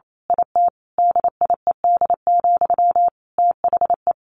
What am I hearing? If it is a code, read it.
Morse code, heard as a keyed tone, about 28 wpm, 700 Hz. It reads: IT DIED, THE
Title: A Sound Of Thunder 28wpm